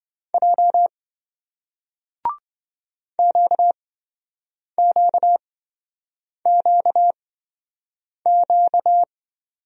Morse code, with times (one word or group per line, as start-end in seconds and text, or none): 0.3-0.9 J
3.2-9.0 QQQQ